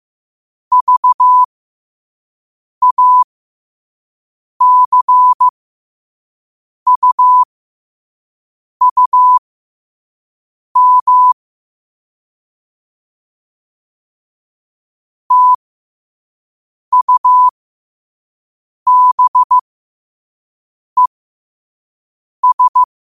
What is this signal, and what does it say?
Morse code, heard as a keyed tone, about 15 wpm, spaced as for 5 wpm, 1000 Hz. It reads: VACUUM TUBES
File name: qotdfiles / 5wpm